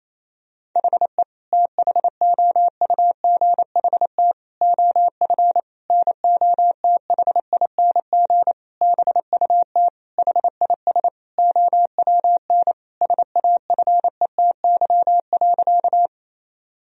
Morse code, heard as a keyed tone, about 28 wpm, 700 Hz. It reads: HE THOUGHT OF NOTHING BUT HIS OWN SAFETY.
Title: Sentences from Top 1000 Words - Review Part 4 - 28wpm